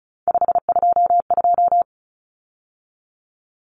Morse code, heard as a keyed tone, about 34 wpm, 700 Hz. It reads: 522
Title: Numbers - 3-Digit 35wpm